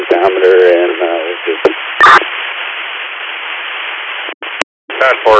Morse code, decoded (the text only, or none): none